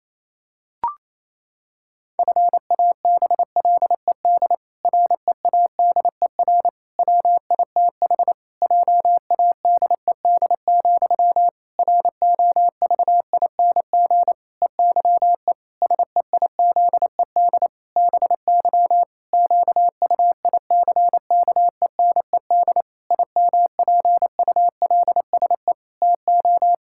Morse code, heard as a keyed tone, about 27 wpm, 700 Hz. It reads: FABLED READER WITH JADED, ROVING EYE SEIZED BY QUICKENED IMPULSE TO
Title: Pangram Challenge v2 - 28wpm